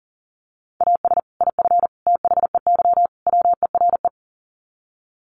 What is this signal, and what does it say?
Morse code, heard as a keyed tone, about 37 wpm, 700 Hz. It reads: AS IF THEY WERE
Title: Sets of 4 Words 40wpm